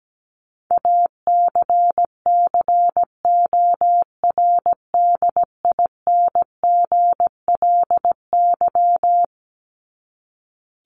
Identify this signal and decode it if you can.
Morse code, heard as a keyed tone, about 17 wpm, 700 Hz. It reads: ACCORDINGLY
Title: LY Suffix - Top 100 Words with Base Words 17wpm